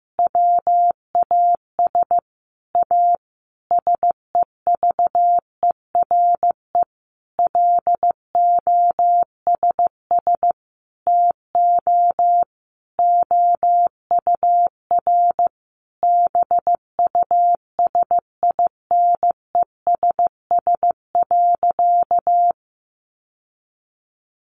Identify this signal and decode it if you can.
Morse code, heard as a keyed tone, about 15 wpm, 700 Hz. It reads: WAS A SEVERE LOSS TO OUR BUSINESS.
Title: Sentences from Top 1400 Words 15wpm